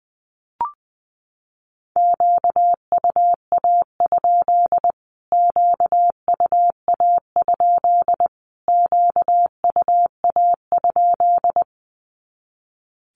Morse code, heard as a keyed tone, about 20 wpm, 700 Hz. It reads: QUA? QUA? QUA?
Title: Q-Codes Repeated 3x 20wpm